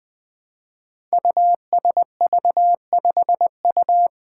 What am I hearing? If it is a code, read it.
Morse code, heard as a keyed tone, about 20 wpm, 700 Hz. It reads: USV5U
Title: Five Challenging Letter-Number Combinations 20wpm